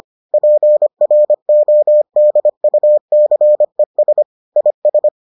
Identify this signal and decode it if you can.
Morse code, heard as a keyed tone, about 25 wpm, 600 Hz. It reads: PRODUCES IS